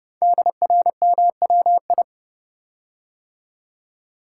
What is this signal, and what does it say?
Morse code, heard as a keyed tone, about 30 wpm, 700 Hz. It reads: DRMWI
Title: Code Groups - 5 Letters - 30wpm